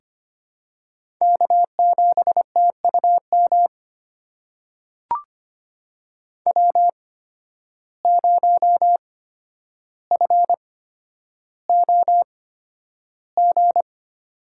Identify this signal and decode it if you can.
Morse code, heard as a keyed tone, about 25 wpm, spaced as for 11 wpm, 700 Hz. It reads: K7TUM W0FOG
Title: USA Call Signs - 5 Characters - 8x Character Spacing - 25wpm